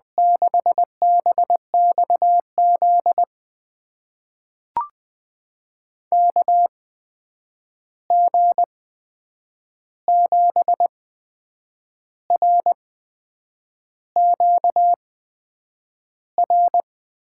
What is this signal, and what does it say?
Morse code, heard as a keyed tone, about 20 wpm, spaced as for 8 wpm, 700 Hz. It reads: 6BXZ KG7RQR
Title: USA Call Signs - 6 Characters - 8x Character Spacing - 20wpm